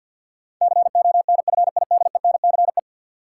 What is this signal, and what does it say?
Morse code, heard as a keyed tone, about 52 wpm, 700 Hz. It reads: CONFIDENCE
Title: Sentences from Top 1500 Words - New Word Drill Alt - 50wpm